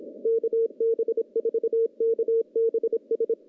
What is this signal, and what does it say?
Morse code, heard as a keyed tone, about 27 wpm, 460 Hz. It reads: KB4KBS
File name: contests / CW-SS-k5tr- -0200